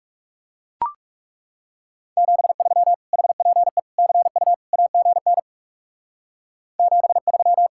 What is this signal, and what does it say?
Morse code, heard as a keyed tone, about 35 wpm, 700 Hz. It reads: 73 HPE CU AGN 73
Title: Speed-Racing Common QSO Phrases - 30wpm